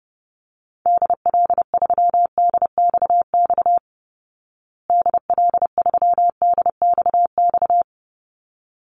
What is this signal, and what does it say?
Morse code, heard as a keyed tone, about 30 wpm, 700 Hz. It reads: DL3DXX DL3DXX
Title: Call Signs Repeated 3x - 30wpm